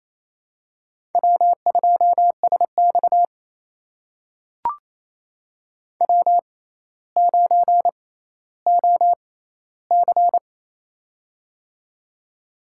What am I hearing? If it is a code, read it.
Morse code, heard as a keyed tone, about 27 wpm, spaced as for 14 wpm, 700 Hz. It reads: W2SX W9OC
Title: USA Call Signs - 4 Characters - 6x Character Spacing - 28wpm